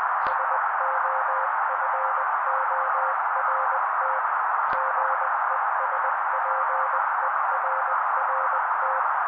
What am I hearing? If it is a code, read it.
Morse code, heard as a keyed tone, about 20 wpm, 530 Hz. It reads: SOFORT GESPERRT
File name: daily / 18wpm noise